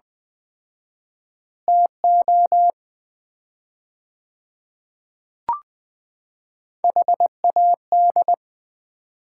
Morse code, heard as a keyed tone, about 20 wpm, 700 Hz. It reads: TO HAD
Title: 4x Word-Spacing - Sets of 4 Words - 20wpm